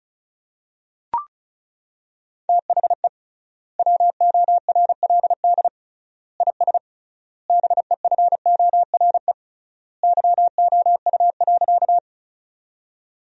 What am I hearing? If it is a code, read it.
Morse code, heard as a keyed tone, about 34 wpm, 700 Hz. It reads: THE WORLD IS BEFORE YOU.
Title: 3x Word-Spacing - Sentences from Top 200 Words - 35wpm